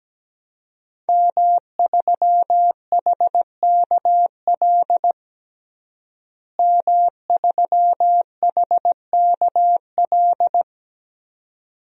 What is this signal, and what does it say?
Morse code, heard as a keyed tone, about 17 wpm, 700 Hz. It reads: M3HKL M3HKL
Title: DXCC-100 Call Signs - 5 Characters - Repeated 2x - 17wpm